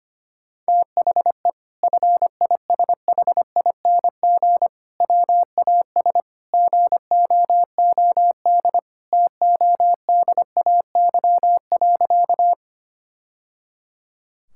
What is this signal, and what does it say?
Morse code, heard as a keyed tone, about 25 wpm, 700 Hz. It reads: THE FISHING WAS GOOD TODAY.
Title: Sentences from Top 1000 Words - Review Part 5 - 25wpm